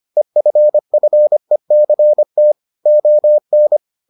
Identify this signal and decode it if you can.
Morse code, heard as a keyed tone, about 25 wpm, 600 Hz. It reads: EFFECT ON